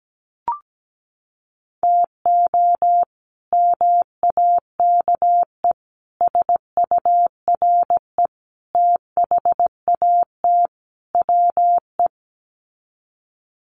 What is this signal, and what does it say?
Morse code, heard as a keyed tone, about 17 wpm, 690 Hz. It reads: TO MAKE SURE THAT WE
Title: Sets of 5 Words Encore 17wpm